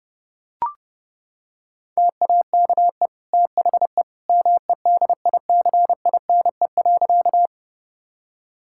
Morse code, heard as a keyed tone, about 30 wpm, 700 Hz. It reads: TAKE THE MEDICINE.
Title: Sentences from Top 1000 Words - Review Part 6 - 30wpm